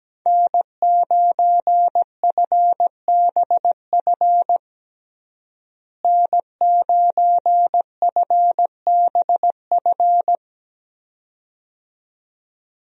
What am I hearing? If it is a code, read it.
Morse code, heard as a keyed tone, about 17 wpm, 700 Hz. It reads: N9FBF N9FBF
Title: USA Call Signs - 5 Characters - Repeated 2x - 17wpm